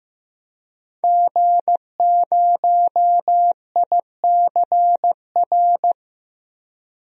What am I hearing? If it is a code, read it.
Morse code, heard as a keyed tone, about 15 wpm, 700 Hz. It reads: G0ICR
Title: DXCC-100 Call Signs - 5 Characters - Repeated 3x - 15wpm